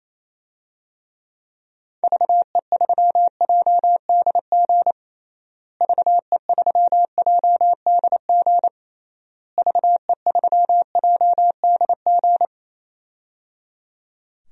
Morse code, heard as a keyed tone, about 28 wpm, 700 Hz. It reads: VE3JDG VE3JDG VE3JDG
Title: DXCC-100 Call Signs - 6 Characters - Repeated 3x - 28wpm